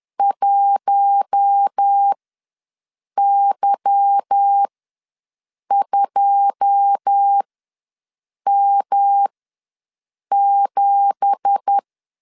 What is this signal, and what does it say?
Morse code, heard as a keyed tone, about 10 wpm, spaced as for 6 wpm, 800 Hz. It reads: 1Y2M7
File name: LEZIONI / 9 2867 ws 10wpm 4215